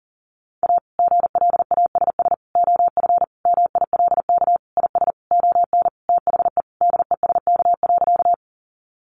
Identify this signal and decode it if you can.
Morse code, heard as a keyed tone, about 40 wpm, 700 Hz. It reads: A GLASS OF MILK IS ON THE DESK.
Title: Additional Easy Phrases 40wpm